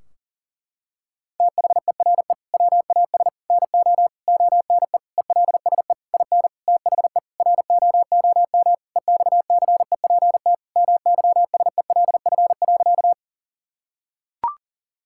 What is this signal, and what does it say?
Morse code, heard as a keyed tone, about 40 wpm, 700 Hz. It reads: THERE WAS NO ONE ELSE IN THE ROOM EXCEPT MYSELF.
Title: Sentences from Top 1600 Words 40wpm